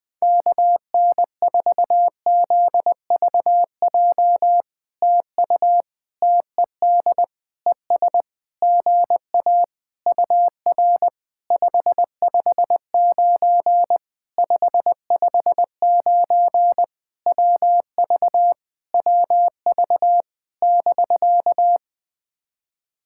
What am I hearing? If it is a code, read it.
Morse code, heard as a keyed tone, about 20 wpm, 700 Hz. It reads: KN4ZVJ TU TED ES GA UR 559 559 WV WV <BK>
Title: POTA QSO Practice - 20wpm